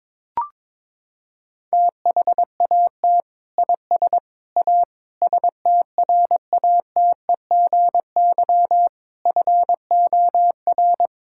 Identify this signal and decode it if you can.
Morse code, heard as a keyed tone, about 22 wpm, 700 Hz. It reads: THAT IS A STRATEGY FOR
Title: Sentences from Top 1000 Words - Review Part 5 - 22wpm